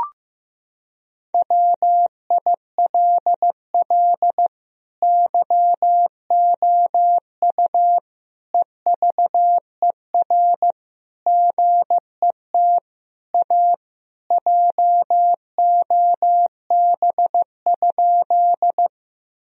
Morse code, heard as a keyed tone, about 16 wpm, 700 Hz. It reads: WILL YOU EVER GET A JOB?
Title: Sentences from Top 1000 Words - Review Part 3 - 15wpm